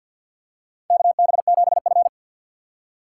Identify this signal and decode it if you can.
Morse code, heard as a keyed tone, about 53 wpm, 690 Hz. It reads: KB7F
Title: USA Call Signs - 4 Characters - 6x Character Spacing - 50wpm